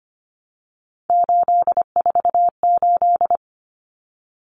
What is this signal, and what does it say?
Morse code, heard as a keyed tone, about 25 wpm, 700 Hz. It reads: 848
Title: Common Cut Numbers - 3-Digit 25wpm